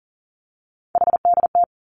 Morse code, heard as a keyed tone, about 42 wpm, 700 Hz. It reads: HDT
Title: Three Challenging Letter-Number Combinations 40wpm